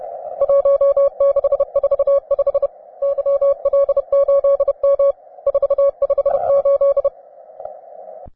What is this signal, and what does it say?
Morse code, heard as a keyed tone, about 31 wpm, 560 Hz. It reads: 1645 YL8M 458